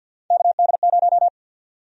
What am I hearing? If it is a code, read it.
Morse code, heard as a keyed tone, about 47 wpm, 700 Hz. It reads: KD0
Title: USA Call Sign Prefixes - 50wpm